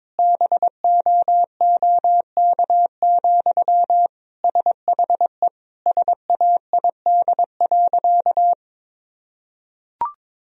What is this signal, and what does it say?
Morse code, heard as a keyed tone, about 22 wpm, 700 Hz. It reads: BOOK, SHE SAID.
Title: Sentences from Top 1000 Words - Review Part 1 - 22wpm